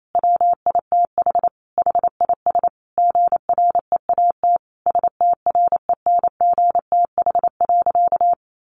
Morse code, heard as a keyed tone, about 28 wpm, 700 Hz. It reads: WITH HIS GREAT STRENGTH.
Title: Sentences from Top 1300 Words 28wpm